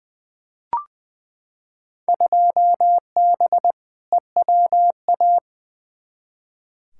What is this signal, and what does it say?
Morse code, heard as a keyed tone, about 20 wpm, 700 Hz. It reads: 2B EWA
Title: Field Day Exchanges - 20wpm